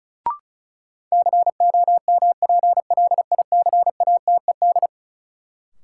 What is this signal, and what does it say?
Morse code, heard as a keyed tone, about 36 wpm, 700 Hz. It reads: COMPLICATED
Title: Sentences from Top 1200 Words - New Word Drill - 35wpm